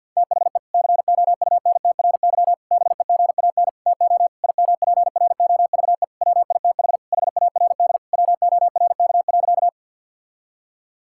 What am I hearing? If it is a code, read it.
Morse code, heard as a keyed tone, about 50 wpm, 700 Hz. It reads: THE COUNTRY BEGAN TO IMPROVE WITH HARD WORK.
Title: Sentences from Top 1000 Words - Review Part 5 - 50wpm